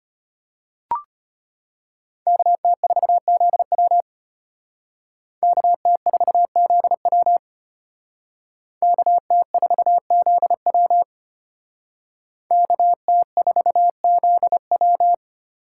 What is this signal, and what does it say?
Morse code, heard as a keyed tone, about 34 wpm, 700 Hz. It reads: KT4ZW KT4ZW KT4ZW KT4ZW